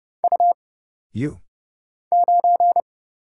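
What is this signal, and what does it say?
Morse code, heard as a keyed tone, about 30 wpm, 700 Hz. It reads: U 9
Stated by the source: Morse Code Ninja single-character practice (letters, numbers, punctuation, prosigns) at 30 wpm, by Kurt Zoglmann (AD0WE).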